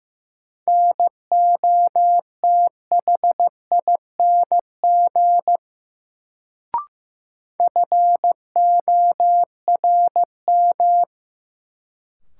Morse code, heard as a keyed tone, about 15 wpm, 700 Hz. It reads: NOTHING FORM
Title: Sets of 4 Words - New Word Drill - 15wpm